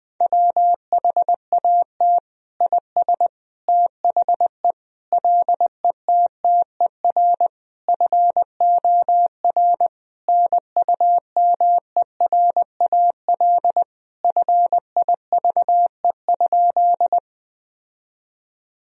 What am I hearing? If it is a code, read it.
Morse code, heard as a keyed tone, about 20 wpm, 700 Hz. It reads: WHAT IS THE LETTER FOR NUMERAL FIVE?
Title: Sentences from Top 1000 Words - Review Part 2 - 20wpm